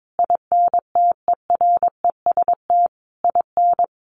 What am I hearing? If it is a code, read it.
Morse code, heard as a keyed tone, about 22 wpm, 700 Hz. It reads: INTEREST IN
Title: Sentences from Top 1300 Words 22wpm